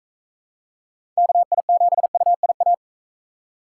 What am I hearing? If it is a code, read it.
Morse code, heard as a keyed tone, about 42 wpm, 700 Hz. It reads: KI7UIA